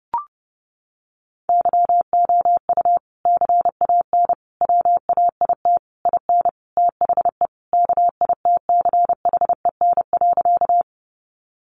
Morse code, heard as a keyed tone, about 30 wpm, 700 Hz. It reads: YOU CAN WAIT IN THE KITCHEN.
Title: Sentences from Top 1400 Words 30wpm